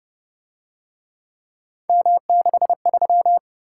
Morse code, heard as a keyed tone, about 30 wpm, 700 Hz. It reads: M63